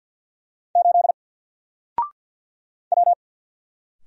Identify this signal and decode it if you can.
Morse code, heard as a keyed tone, about 48 wpm, 700 Hz. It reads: 8 W
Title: Single Character (Letter:Number:Punctuation:Prosign) 50wpm